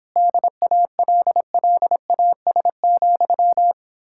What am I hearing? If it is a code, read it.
Morse code, heard as a keyed tone, about 29 wpm, 700 Hz. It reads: DALLAS,